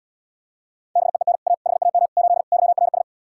Morse code, heard as a keyed tone, about 79 wpm, 700 Hz. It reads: BEEN A HAM 8 YRS